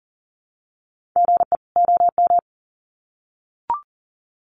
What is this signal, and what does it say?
Morse code, heard as a keyed tone, about 40 wpm, 700 Hz. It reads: GE OM
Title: Common QSO Phrases 40wpm